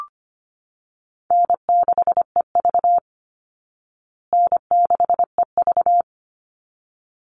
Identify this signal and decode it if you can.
Morse code, heard as a keyed tone, about 25 wpm, 700 Hz. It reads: N6EV N6EV
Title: Easy Call Signs Repeated 2x - 25wpm